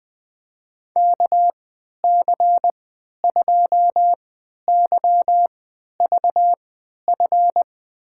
Morse code, heard as a keyed tone, about 20 wpm, spaced as for 11 wpm, 700 Hz. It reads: KC2YVF
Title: USA Call Signs - 3x Character Spacing - 20wpm